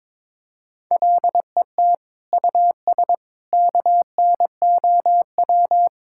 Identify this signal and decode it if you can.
Morse code, heard as a keyed tone, about 22 wpm, 700 Hz. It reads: LET US KNOW
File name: practice / 22wpm / Sets of 6 Words - 22wpm